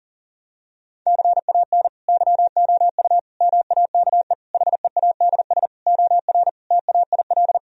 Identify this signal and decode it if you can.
Morse code, heard as a keyed tone, about 40 wpm, 700 Hz. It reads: CAN YOU MAKE HEADS OR TAIL
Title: Sentences from Top 1000 Words - Review Part 3 - 40wpm